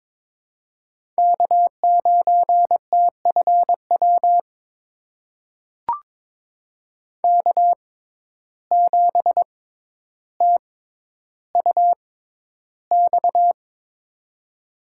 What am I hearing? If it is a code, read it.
Morse code, heard as a keyed tone, about 22 wpm, spaced as for 11 wpm, 700 Hz. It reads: K9TFW K7TUX
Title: USA Call Signs - 5 Characters - 6x Character Spacing - 22wpm